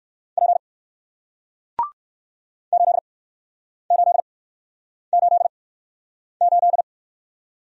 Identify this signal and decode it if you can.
Morse code, heard as a keyed tone, about 60 wpm, spaced as for 9 wpm, 700 Hz. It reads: P 8888